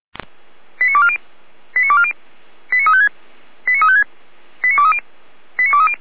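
Five-tone selective call (ZVEI), heard as quick runs of five tones, tones of about 70 ms. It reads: 89240 89240 89356 89356 89230 89230
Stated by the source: ZVEI five-tone call 89150 received on an SDR.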